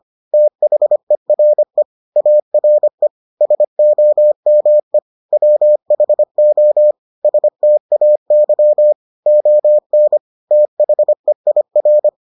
Morse code, heard as a keyed tone, about 25 wpm, 600 Hz. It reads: THERE ARE SOME WHO STAY ON THEIR